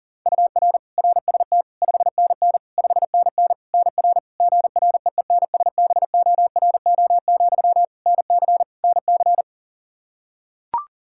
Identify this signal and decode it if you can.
Morse code, heard as a keyed tone, about 40 wpm, 700 Hz. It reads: UR RST 5NN 5NN NR GREENSBORO, NC NC
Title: Practice QSO - Exchange 1 - Beginning - 40wpm